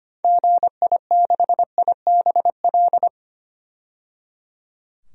Morse code, heard as a keyed tone, about 25 wpm, 700 Hz. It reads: GI6IBL
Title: DXCC-100 Call Signs - 6 Characters - 25wpm